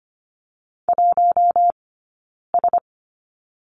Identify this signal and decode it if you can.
Morse code, heard as a keyed tone, about 25 wpm, spaced as for 9 wpm, 690 Hz. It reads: 1S
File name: Lesson 15d - Number 5 - Two Characters 25wpm